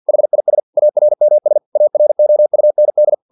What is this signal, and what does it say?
Morse code, heard as a keyed tone, about 50 wpm, 600 Hz. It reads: HIS ARMS AROUND
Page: Never 49wpm